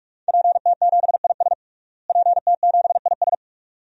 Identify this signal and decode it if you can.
Morse code, heard as a keyed tone, about 46 wpm, 700 Hz. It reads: PT7IS PT7IS
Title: DXCC-100 Call Signs - 5 Characters - Repeated 2x - 45wpm